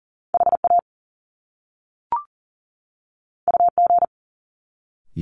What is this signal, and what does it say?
Morse code, heard as a keyed tone, about 39 wpm, 700 Hz. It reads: HA UG
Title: Top Two Letter Combinations in Words 40wpm